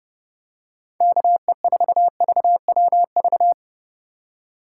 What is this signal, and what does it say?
Morse code, heard as a keyed tone, about 28 wpm, 700 Hz. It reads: KE4VWV